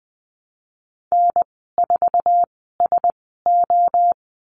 Morse code, heard as a keyed tone, about 20 wpm, spaced as for 15 wpm, 700 Hz. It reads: N4SO
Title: 055 Lesson 10g - Letter r - Call Signs 20wpm